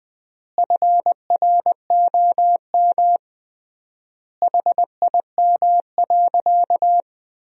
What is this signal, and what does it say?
Morse code, heard as a keyed tone, about 21 wpm, 700 Hz. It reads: FROM HIM.
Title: 3x Word-Spacing - Sentences from Top 400 Words - 20wpm